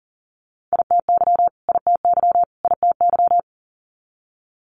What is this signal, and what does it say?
Morse code, heard as a keyed tone, about 39 wpm, 700 Hz. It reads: ITY ITY ITY
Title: Multi-letter Phonemes Repeated 3x 40wpm